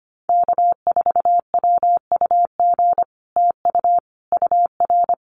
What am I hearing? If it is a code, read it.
Morse code, heard as a keyed tone, about 25 wpm, 700 Hz. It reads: K4WUG TU UR
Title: Common QSO Phrases Encore 25wpm